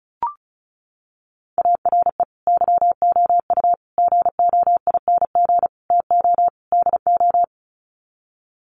Morse code, heard as a keyed tone, about 34 wpm, 700 Hz. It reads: ARE YOU GOING TO DO
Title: Sets of 5 Words 35wpm